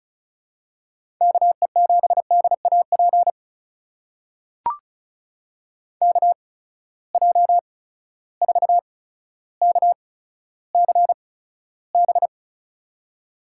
Morse code, heard as a keyed tone, about 36 wpm, spaced as for 15 wpm, 700 Hz. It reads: KE7DAP KJ4KCB
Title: USA Call Signs - 6 Characters - 8x Character Spacing - 35wpm